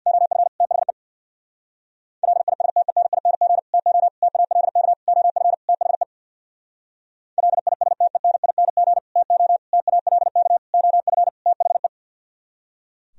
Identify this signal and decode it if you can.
Morse code, heard as a keyed tone, about 64 wpm, 700 Hz. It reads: OF THE LISTENING TO TALK OF THE LISTENING TO TALK OF THE